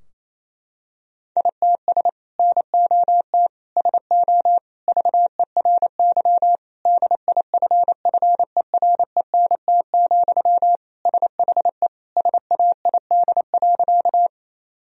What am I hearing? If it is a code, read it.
Morse code, heard as a keyed tone, about 28 wpm, 700 Hz. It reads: ITS NOT SO VERY DIFFERENT, SHE SAID.
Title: Sentences from Top 1000 Words - Review Part 3 - 28wpm